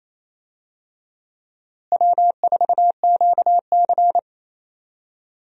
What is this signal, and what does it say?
Morse code, heard as a keyed tone, about 28 wpm, 700 Hz. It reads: W4QC
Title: USA Call Signs - 4 Characters - Repeated 2x - 28wpm